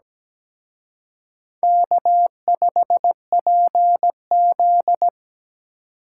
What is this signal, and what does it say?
Morse code, heard as a keyed tone, about 20 wpm, 700 Hz. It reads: K5PZ
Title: Speed Racing - USA 1x2 Call Signs 17wpm